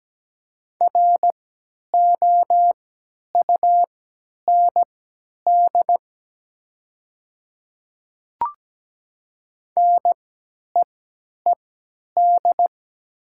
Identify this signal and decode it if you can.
Morse code, heard as a keyed tone, about 17 wpm, spaced as for 10 wpm, 700 Hz. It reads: ROUND NEED
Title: 3x Character Spacing - Top 500 Words - 17wpm